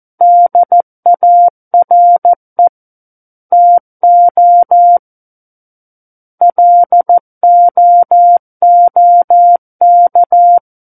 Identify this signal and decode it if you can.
Morse code, heard as a keyed tone, about 14 wpm, 700 Hz. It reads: DARE TO LOOK
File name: Book One / 05-Chapter 5 - 14 WPM